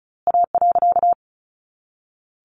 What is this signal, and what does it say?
Morse code, heard as a keyed tone, about 34 wpm, 700 Hz. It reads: A.
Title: A Sound Of Thunder 35wpm